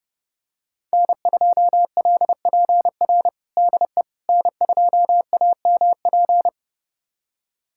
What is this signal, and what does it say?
Morse code, heard as a keyed tone, about 30 wpm, 700 Hz. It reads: N2LPR DE N2AMP
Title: USA Callsign DE Callsign - 5 Characters - 30wpm